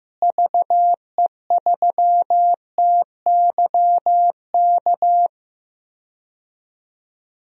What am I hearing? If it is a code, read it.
Morse code, heard as a keyed tone, about 16 wpm, 700 Hz. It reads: VE3TYK
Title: DXCC-100 Call Signs - 6 Characters - Repeated 3x - 15wpm